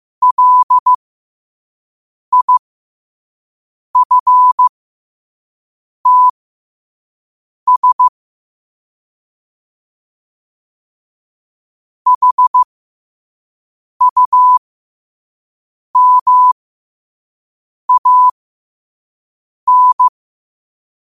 Morse code, heard as a keyed tone, about 15 wpm, spaced as for 5 wpm, 1000 Hz. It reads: LIFTS HUMAN
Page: New quotes every day in morse code at 5 Words per minute.